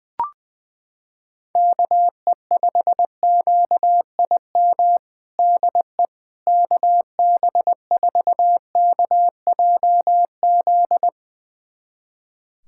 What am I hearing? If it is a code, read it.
Morse code, heard as a keyed tone, about 20 wpm, 700 Hz. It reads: KE5QIM DE KB4KJZ
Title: USA Callsign DE Callsign - 6 Characters - 20wpm